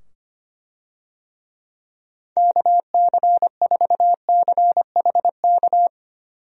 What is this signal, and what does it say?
Morse code, heard as a keyed tone, about 25 wpm, 700 Hz. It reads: KC4CHK
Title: USA Call Signs - 6 Characters - 25wpm